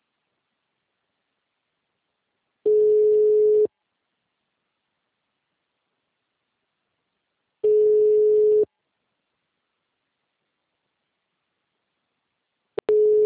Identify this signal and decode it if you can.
Ringback tone (Europe, 425 Hz).
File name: out- -2004- - - .7